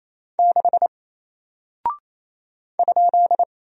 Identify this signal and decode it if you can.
Morse code, heard as a keyed tone, about 29 wpm, 700 Hz. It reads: 6 ?
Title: Single Character (Letter:Number:Punctuation:Prosign) Alt 28wpm